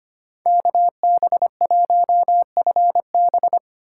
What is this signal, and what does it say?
Morse code, heard as a keyed tone, about 25 wpm, 700 Hz. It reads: KB1FB
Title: USA Call Signs - 5 Characters - 2x Character Spacing - 25wpm